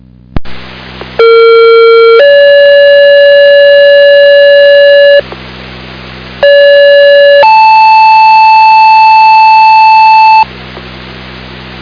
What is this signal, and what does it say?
Two-tone page, heard as a one-second tone, then a three-second tone, about 460 then 570 Hz, about 570 then 880 Hz. It is heard from 1.2 to 10.4 s.